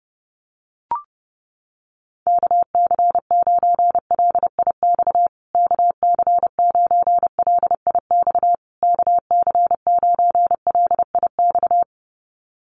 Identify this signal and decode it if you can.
Morse code, heard as a keyed tone, about 30 wpm, 700 Hz. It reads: KC9LIX KC9LIX KC9LIX
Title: USA Call Signs Repeated 3x 30wpm